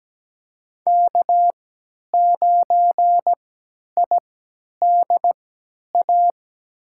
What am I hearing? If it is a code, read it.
Morse code, heard as a keyed tone, about 17 wpm, spaced as for 10 wpm, 700 Hz. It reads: K9IDA
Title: USA Call Signs - 5 Characters - 3x Character Spacing - 17wpm